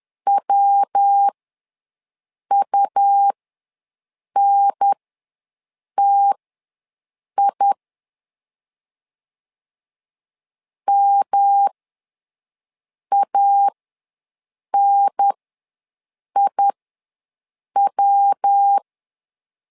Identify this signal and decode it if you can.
Morse code, heard as a keyed tone, about 10 wpm, spaced as for 6 wpm, 800 Hz. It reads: WUNTI MANIW
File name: LEZIONI / 3 MWRO / 3 MW ws 10wpm 2213